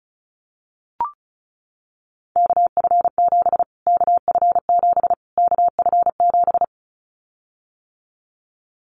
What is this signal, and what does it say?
Morse code, heard as a keyed tone, about 34 wpm, 700 Hz. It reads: KF7 KF7 KF7
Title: Callsign Prefixes for Top 25 DXCC - Repeated 3x - 35wpm